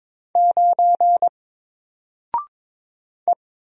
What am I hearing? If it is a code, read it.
Morse code, heard as a keyed tone, about 22 wpm, 700 Hz. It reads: 9 E
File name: Single-Letter-Number-22wpm